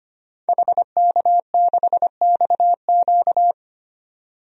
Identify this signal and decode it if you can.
Morse code, heard as a keyed tone, about 25 wpm, 700 Hz. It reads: HK6XQ
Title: DXCC-100 Call Signs - 5 Characters - 25wpm